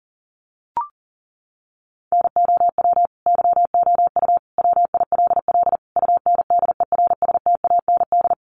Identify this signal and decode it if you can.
Morse code, heard as a keyed tone, about 40 wpm, 700 Hz. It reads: NOW YOU WILL UNDERSTAND
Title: Sentences from Top 1200 Words 40wpm